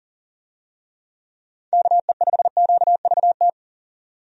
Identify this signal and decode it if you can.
Morse code, heard as a keyed tone, about 40 wpm, 700 Hz. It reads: KE5QVT
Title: USA Call Signs - 6 Characters - 40wpm